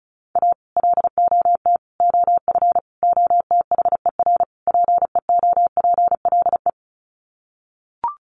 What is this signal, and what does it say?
Morse code, heard as a keyed tone, about 34 wpm, 700 Hz. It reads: A LOT OF OTHER PEOPLE
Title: Sets of 5 Words Encore2 35wpm